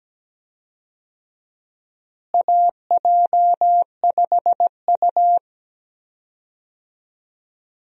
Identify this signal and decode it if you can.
Morse code, heard as a keyed tone, about 17 wpm, 700 Hz. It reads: AJ5U
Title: USA Call Signs - 4 Characters - Repeated 3x - 17wpm